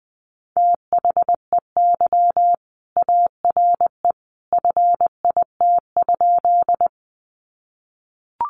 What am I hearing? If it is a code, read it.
Morse code, heard as a keyed tone, about 20 wpm, 700 Hz. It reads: THEY ARE FIT?
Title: Sentences from Top 800 Words 20wpm